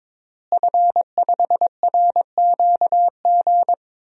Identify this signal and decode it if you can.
Morse code, heard as a keyed tone, about 22 wpm, 700 Hz. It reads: F5RQG
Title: DXCC-100 Call Signs - 5 Characters - Repeated 3x - 22wpm